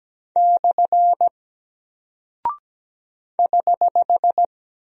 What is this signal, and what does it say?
Morse code, heard as a keyed tone, about 17 wpm, 700 Hz. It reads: / <HH>
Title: Single Character (Letter:Number:Punctuation:Prosign) Alt 17wpm